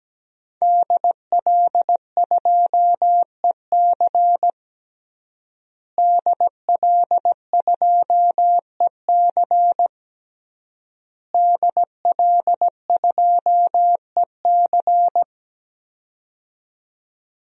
Morse code, heard as a keyed tone, about 17 wpm, 700 Hz. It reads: DL2EC DL2EC DL2EC
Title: DXCC-100 Call Signs - 5 Characters - Repeated 3x - 17wpm